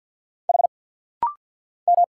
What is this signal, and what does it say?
Morse code, heard as a keyed tone, about 50 wpm, 690 Hz. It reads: H M
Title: Single Letters - Rapid-Fire 50wpm